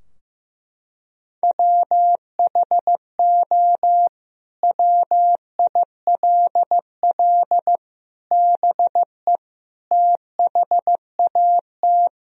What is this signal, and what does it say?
Morse code, heard as a keyed tone, about 16 wpm, 700 Hz. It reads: WHO WILL BE THAT
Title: Sentences from Top 1000 Words - Review Part 4 - 15wpm